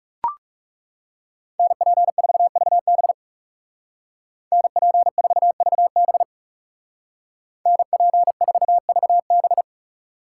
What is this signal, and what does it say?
Morse code, heard as a keyed tone, about 37 wpm, 700 Hz. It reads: NP4VB NP4VB NP4VB
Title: Speed Racing - USA 2x2 Call Signs 30wpm